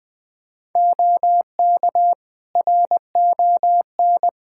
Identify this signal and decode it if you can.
Morse code, heard as a keyed tone, about 21 wpm, 700 Hz. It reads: OK RON
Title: Practice QSO - Exchange 3 - More Info - 20wpm